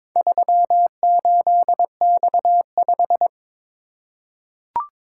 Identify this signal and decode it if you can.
Morse code, heard as a keyed tone, about 23 wpm, 700 Hz. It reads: 38X5
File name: practice / 22wpm / Four Letter-Number Combinations - 22wpm